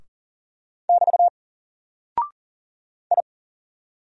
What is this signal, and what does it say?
Morse code, heard as a keyed tone, about 42 wpm, 700 Hz. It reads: <BT> I
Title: Single Character (Letter:Number:Punctuation:Prosign) 40wpm